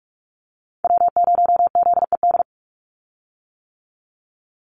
Morse code, heard as a keyed tone, about 44 wpm, 700 Hz. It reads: W0ZED
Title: USA Call Signs 45wpm